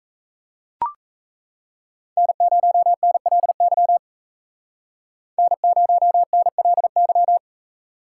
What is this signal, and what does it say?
Morse code, heard as a keyed tone, about 34 wpm, 700 Hz. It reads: N0NLY N0NLY
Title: Speed Racing - USA 1x3 Call Signs 28wpm